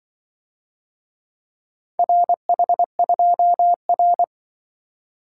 Morse code, heard as a keyed tone, about 25 wpm, 700 Hz. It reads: RH2R
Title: Speed-Racing - Four Letter-Number Combinations - 20wpm